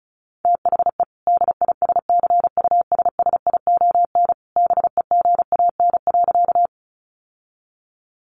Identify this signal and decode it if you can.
Morse code, heard as a keyed tone, about 34 wpm, 700 Hz. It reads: THE DISCUSSION BEGAN.
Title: Sentences from Top 1100 Words 35wpm